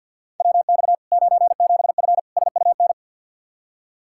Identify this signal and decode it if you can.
Morse code, heard as a keyed tone, about 49 wpm, 700 Hz. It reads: WX 97F SUN
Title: Practice QSO - Exchange 2 - Info - 50wpm